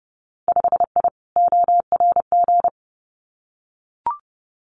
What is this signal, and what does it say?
Morse code, heard as a keyed tone, about 30 wpm, 700 Hz. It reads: 5I ORG
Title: Winter Field Day Exchanges Encore 30wpm